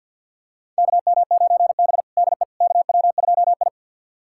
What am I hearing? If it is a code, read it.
Morse code, heard as a keyed tone, about 47 wpm, 700 Hz. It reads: KM9B DE KW2I
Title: USA Callsign DE Callsign - 4 Characters - 50wpm